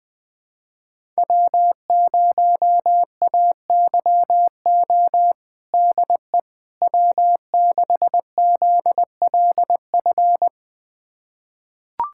Morse code, heard as a keyed tone, about 21 wpm, 700 Hz. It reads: W0AYO DE W6ZLF
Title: USA Callsign DE Callsign - 5 Characters - 20wpm